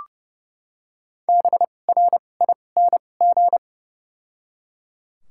Morse code, heard as a keyed tone, about 30 wpm, spaced as for 22 wpm, 700 Hz. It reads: BRING
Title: 2x Character Spacing - Top 500 Words - 30wpm